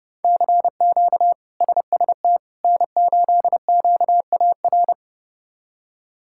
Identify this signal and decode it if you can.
Morse code, heard as a keyed tone, about 32 wpm, 700 Hz. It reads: CQ SST N8QAR
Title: K1USN SST QSO Practice - 30wpm